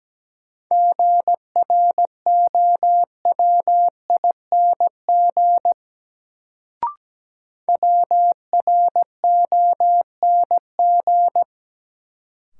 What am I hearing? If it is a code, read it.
Morse code, heard as a keyed tone, about 17 wpm, 700 Hz. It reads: GROWING WRONG
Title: Sets of 5 Words Encore - New Word Drill - 17wpm